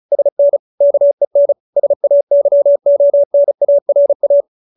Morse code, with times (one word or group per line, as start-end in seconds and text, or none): 0.1-0.6 SN
0.8-1.5 KEN
1.8-4.4 SAYONARA